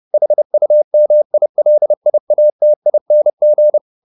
Morse code, heard as a keyed tone, about 30 wpm, 600 Hz. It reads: HUMILIATING
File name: the old man and the sea / the old man and the sea 1 30wpm